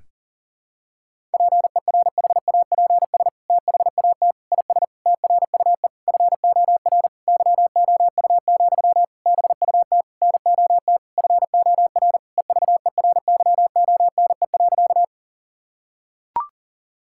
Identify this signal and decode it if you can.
Morse code, heard as a keyed tone, about 40 wpm, 700 Hz. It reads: PERHAPS THAT IS TRUE FOR YOU, BUT NOT FOR EVERYONE.
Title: Sentences from Top 1000 Words - Review Part 5 - 40wpm